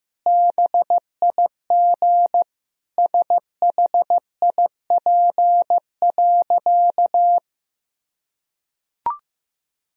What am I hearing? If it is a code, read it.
Morse code, heard as a keyed tone, about 15 wpm, 700 Hz. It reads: BIG SHIP.
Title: Sentences from Top 1000 Words - Review Part 2 - 15wpm